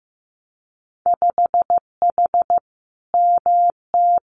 Morse code, heard as a keyed tone, about 15 wpm, 700 Hz. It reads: 5H MT
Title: Winter Field Day Exchanges Encore 15wpm